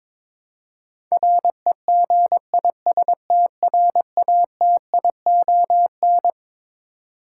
Morse code, heard as a keyed tone, about 22 wpm, 700 Hz. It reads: REGISTRATION
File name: Twelve Letter Words - 22wpm